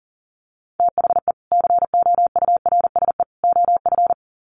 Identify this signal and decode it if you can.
Morse code, heard as a keyed tone, about 40 wpm, 700 Hz. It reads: THE COURSE OF
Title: Sets of 3 Words 40wpm